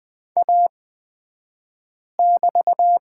Morse code, heard as a keyed tone, about 20 wpm, spaced as for 5 wpm, 700 Hz. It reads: A<BT>
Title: 235 Lesson 38d - Letter q - Two Characters 20wpm